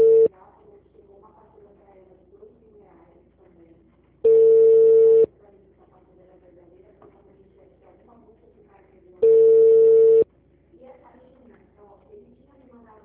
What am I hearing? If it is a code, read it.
Ringback tone (China, 450 Hz), with speech.